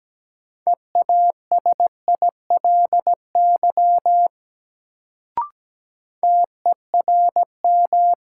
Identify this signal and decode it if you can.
Morse code, heard as a keyed tone, about 17 wpm, 700 Hz. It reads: EASILY TERM
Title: Sentences from Top 900 Words - New Word Drill - 17wpm